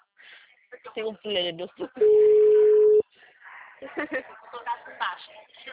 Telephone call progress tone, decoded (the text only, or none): ringback tone